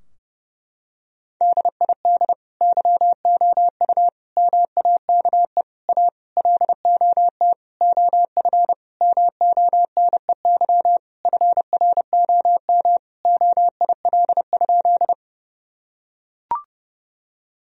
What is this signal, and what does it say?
Morse code, heard as a keyed tone, about 30 wpm, 700 Hz. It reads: DID YOU MAKE A LOT OF MONEY FROM OIL?
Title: Sentences from Top 1000 Words - Review Part 4 - 30wpm